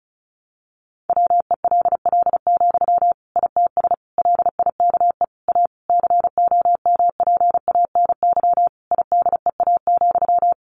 Morse code, heard as a keyed tone, about 34 wpm, 700 Hz. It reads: WELL, ITS LIKE A COMPANY IDEA,
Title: Sentences from Top 600 Words 35wpm